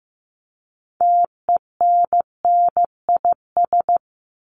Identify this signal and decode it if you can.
Morse code, heard as a keyed tone, about 15 wpm, 700 Hz. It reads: TENNIS
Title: Top 1501-1750 Words 15wpm